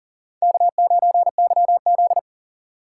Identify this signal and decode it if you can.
Morse code, heard as a keyed tone, about 40 wpm, 700 Hz. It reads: K9YZ
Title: USA Call Signs - 4 Characters - 6x Character Spacing - 40wpm